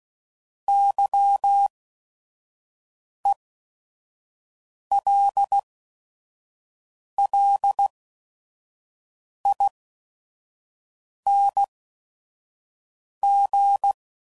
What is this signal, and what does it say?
Morse code, heard as a keyed tone, about 16 wpm, spaced as for 5 wpm, 800 Hz. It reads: YELLING
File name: Suffixes / Special ING's / VerySlowToFaster ING's